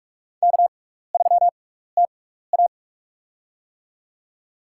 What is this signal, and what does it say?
Morse code, heard as a keyed tone, about 44 wpm, spaced as for 15 wpm, 700 Hz. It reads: K3TA
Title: USA Call Signs - 6x Character Spacing - 45wpm